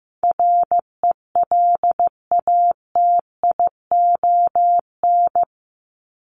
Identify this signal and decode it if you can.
Morse code, heard as a keyed tone, about 15 wpm, 700 Hz. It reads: RELATION
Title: ES:S Suffix - Top 100 Words with Base Words 15wpm